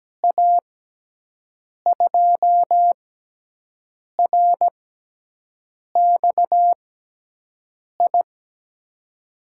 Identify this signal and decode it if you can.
Morse code, heard as a keyed tone, about 17 wpm, spaced as for 6 wpm, 700 Hz. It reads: A2RXI
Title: USA Call Signs - 6 Characters - 6x Character Spacing - 17wpm